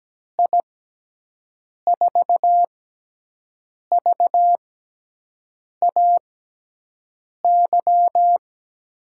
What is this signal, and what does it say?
Morse code, heard as a keyed tone, about 17 wpm, spaced as for 6 wpm, 700 Hz. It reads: I4VAY
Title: USA Call Signs - 6x Character Spacing - 17wpm